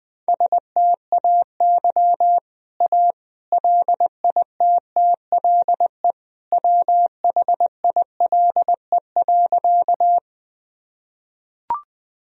Morse code, heard as a keyed tone, about 20 wpm, 700 Hz. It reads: STAY A LITTLE WHILE.
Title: Sentences from Top 1000 Words - Review Part 1 - 20wpm